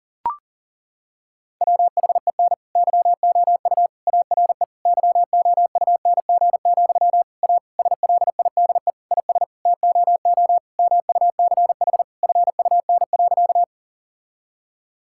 Morse code, heard as a keyed tone, about 40 wpm, 700 Hz. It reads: WHEN YOU ARE YOUNG, A SLIDE IS TOO MUCH FUN.
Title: Sentences from Top 1600 Words 40wpm